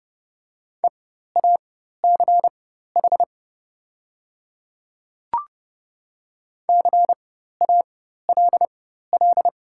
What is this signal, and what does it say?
Morse code, heard as a keyed tone, about 28 wpm, spaced as for 14 wpm, 690 Hz. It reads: EACH CALL
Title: 4x Character Spacing - Top 200 Words - 30wpm